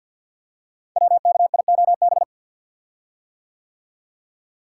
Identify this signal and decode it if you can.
Morse code, heard as a keyed tone, about 53 wpm, 700 Hz. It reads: WKIOB
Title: Code Groups - 5 Letters - 50wpm